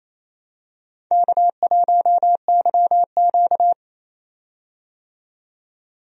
Morse code, heard as a keyed tone, about 28 wpm, 700 Hz. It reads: K1YQ
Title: USA Call Signs - 4 Characters - 28wpm